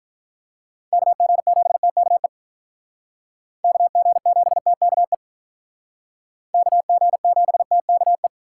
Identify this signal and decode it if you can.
Morse code, heard as a keyed tone, about 46 wpm, 700 Hz. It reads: KG7TKE KG7TKE KG7TKE